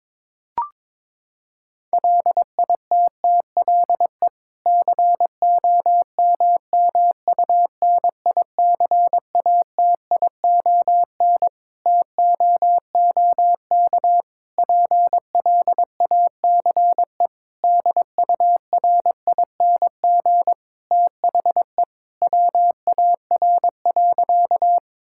Morse code, heard as a keyed tone, about 22 wpm, 700 Hz. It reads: LITTLE COMMUNICATION TOOK PLACE DURING THE WAR.
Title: Sentences from Top 1000 Words - Review Part 6 - 22wpm